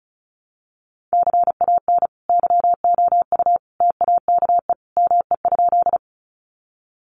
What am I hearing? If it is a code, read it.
Morse code, heard as a keyed tone, about 34 wpm, 700 Hz. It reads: CAN YOU TAKE ME?
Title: Sentences from Top 200 Words 35wpm